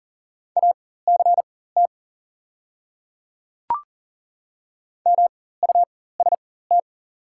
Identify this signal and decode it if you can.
Morse code, heard as a keyed tone, about 40 wpm, spaced as for 19 wpm, 700 Hz. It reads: ACT MUST
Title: 4x Character Spacing - Top 200 Words - 40wpm